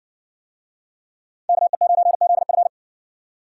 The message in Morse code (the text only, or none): KE9ZF